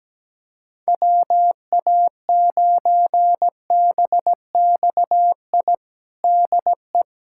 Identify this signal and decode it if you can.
Morse code, heard as a keyed tone, about 17 wpm, 700 Hz. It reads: WA9BXI DE
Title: Practice QSO - Exchange 4 - Ending - 17wpm